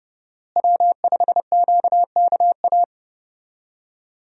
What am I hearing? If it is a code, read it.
Morse code, heard as a keyed tone, about 30 wpm, 700 Hz. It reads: W5QKA
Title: USA Call Signs - 5 Characters - Repeated 2x - 30wpm